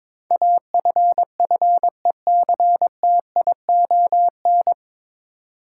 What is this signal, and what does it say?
Morse code, heard as a keyed tone, about 22 wpm, 700 Hz. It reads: AFFECTION
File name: 9 Letter Words - First 3 Letters Given - 22wpm